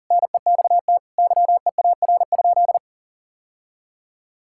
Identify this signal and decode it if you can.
Morse code, heard as a keyed tone, about 37 wpm, 700 Hz. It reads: NEXT YEAR?
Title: Sentences from Top 1000 Words - Review Part 5 - 40wpm